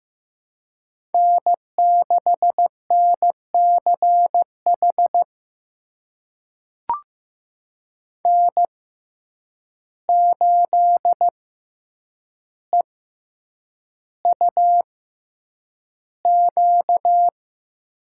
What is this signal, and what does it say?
Morse code, heard as a keyed tone, about 15 wpm, spaced as for 8 wpm, 700 Hz. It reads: N6NCH N8EUQ
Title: USA Call Signs - 5 Characters - 6x Character Spacing - 15wpm